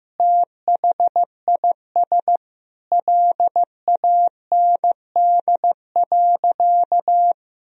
Morse code, heard as a keyed tone, about 15 wpm, 700 Hz. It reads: THIS LAND.
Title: Sentences from Top 1000 Words - Review Part 4 - 15wpm